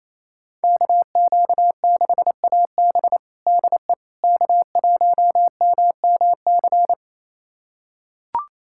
Morse code, heard as a keyed tone, about 27 wpm, 700 Hz. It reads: KQ6AB DE K1MMC
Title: USA Callsign DE Callsign - 5 Characters - 28wpm